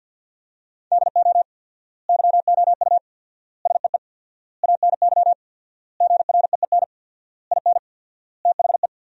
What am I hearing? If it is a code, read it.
Morse code, heard as a keyed tone, about 52 wpm, 700 Hz. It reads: DO YOU SEE ANY GREEN IN THE
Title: 4x Word-Spacing - Sentences from Top 500 Words - 50wpm